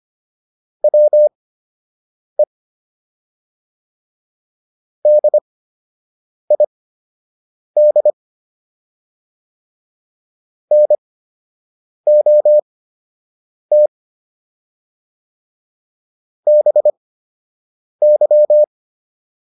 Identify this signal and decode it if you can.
Morse code, heard as a keyed tone, about 25 wpm, spaced as for 7 wpm, 600 Hz. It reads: WE DID NOT BY